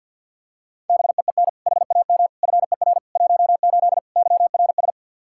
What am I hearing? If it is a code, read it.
Morse code, heard as a keyed tone, about 47 wpm, 700 Hz. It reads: BEEN HAM FER 18 YRS
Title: Practice QSO - Exchange 3 - More Info - 50wpm